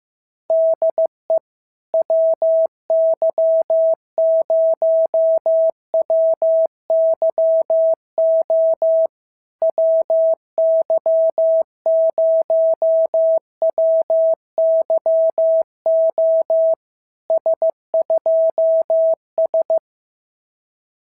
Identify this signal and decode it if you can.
Morse code, heard as a keyed tone, about 15 wpm, 650 Hz. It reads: DE WY0WYO WY0WYO S2S
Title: SOTA QSO Practice 15wpm